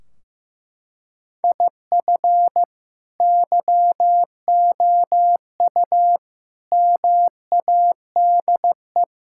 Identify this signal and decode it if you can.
Morse code, heard as a keyed tone, about 16 wpm, 700 Hz. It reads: IF YOU MADE
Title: Sentences from Top 1600 Words 15wpm